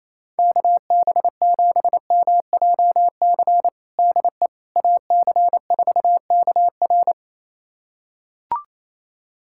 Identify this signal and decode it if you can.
Morse code, heard as a keyed tone, about 28 wpm, 700 Hz. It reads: KB7MJC DE AC4KR
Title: USA Callsign DE Callsign 28wpm